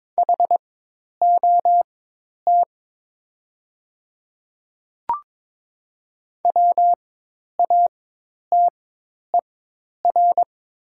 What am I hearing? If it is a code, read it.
Morse code, heard as a keyed tone, about 22 wpm, spaced as for 10 wpm, 690 Hz. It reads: HOT WATER
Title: 4x Character Spacing - Top 300 Words - 22wpm